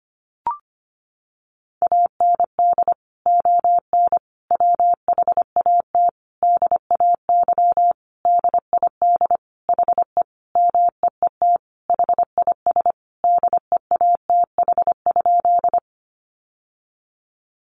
Morse code, heard as a keyed tone, about 25 wpm, 700 Hz. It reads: AND ON WHAT DAY DID HE MEET HIS DEATH?
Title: Sentences from Top 1100 Words 25wpm